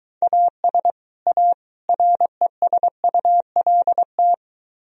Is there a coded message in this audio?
Yes: Morse code from 0.2 to 4.3 s, heard as a keyed tone, about 23 wpm, 700 Hz. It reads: AS A RESULT